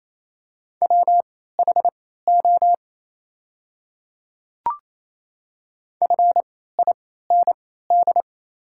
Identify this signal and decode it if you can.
Morse code, heard as a keyed tone, about 28 wpm, spaced as for 16 wpm, 700 Hz. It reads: WHO FIND
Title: 3x Character Spacing - Top 100 Words - 28wpm